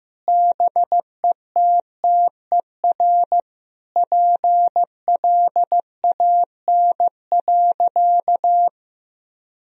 Morse code, heard as a keyed tone, about 15 wpm, 700 Hz. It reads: BETTER PLAN.
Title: Sentences from Top 1000 Words - Review Part 2 - 15wpm